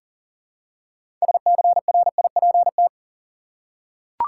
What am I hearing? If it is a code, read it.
Morse code, heard as a keyed tone, about 40 wpm, 700 Hz. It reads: SCRIPT
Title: Six Letter Words - 40wpm